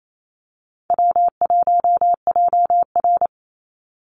Morse code, heard as a keyed tone, about 28 wpm, 700 Hz. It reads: W1JR
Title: Easy Call Signs Repeated 2x - 28wpm